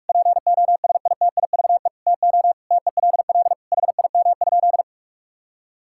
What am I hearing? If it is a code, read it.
Morse code, heard as a keyed tone, about 46 wpm, 700 Hz. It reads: POSITIVE TO TELL HIM?
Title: Sentences from Top 1000 Words - Review Part 5 - 45wpm